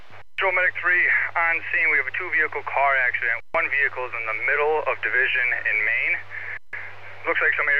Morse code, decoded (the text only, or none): none